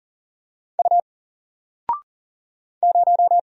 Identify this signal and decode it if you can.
Morse code, heard as a keyed tone, about 40 wpm, 700 Hz. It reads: U 0
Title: Single Letter-Number 40wpm